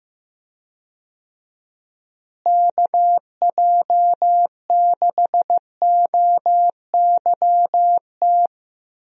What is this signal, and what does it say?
Morse code, heard as a keyed tone, about 15 wpm, 700 Hz. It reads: KJ6OYT
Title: USA Call Signs - 6 Characters - Repeated 2x - 15wpm